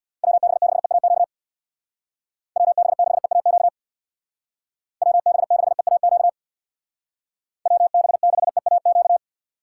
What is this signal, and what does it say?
Morse code, heard as a keyed tone, about 65 wpm, 700 Hz. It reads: WB6EAQ WB6EAQ WB6EAQ WB6EAQ